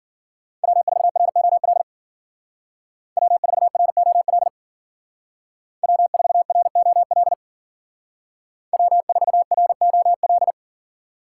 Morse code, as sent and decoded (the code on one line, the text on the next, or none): .-- ....- .-. --- .-.. / .-- ....- .-. --- .-.. / .-- ....- .-. --- .-.. / .-- ....- .-. --- .-..
W4ROL W4ROL W4ROL W4ROL